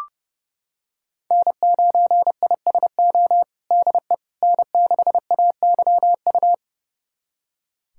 Morse code, heard as a keyed tone, about 30 wpm, 700 Hz. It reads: N9ISO DE N6AYU
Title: USA Callsign DE Callsign - 5 Characters - 30wpm